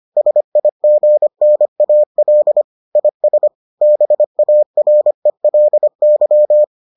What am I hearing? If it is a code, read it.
Morse code, heard as a keyed tone, about 25 wpm, 600 Hz. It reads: SIGNAL IS BARELY